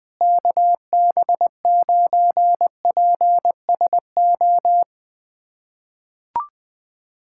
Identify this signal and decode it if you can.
Morse code, heard as a keyed tone, about 20 wpm, 700 Hz. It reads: KB9PSO
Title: USA Callsign DE Callsign 20wpm